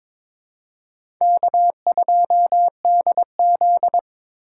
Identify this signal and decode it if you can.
Morse code, heard as a keyed tone, about 22 wpm, 700 Hz. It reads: K2DZ
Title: USA Call Signs - 4 Characters - 3x Character Spacing - 22wpm